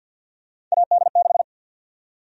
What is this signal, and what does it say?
Morse code, heard as a keyed tone, about 50 wpm, 700 Hz. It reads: AD6
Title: USA Call Sign Prefixes - 50wpm